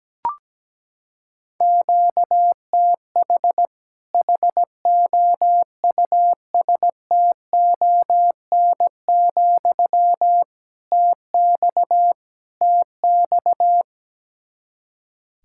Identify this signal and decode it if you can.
Morse code, heard as a keyed tone, about 17 wpm, 700 Hz. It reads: QTH HOUSTON, TX TX
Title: Practice QSO - Exchange 1 - Beginning - 17wpm